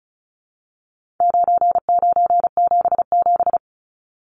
Morse code, heard as a keyed tone, about 34 wpm, 700 Hz. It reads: 9977
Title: Numbers - 4-Digit 35wpm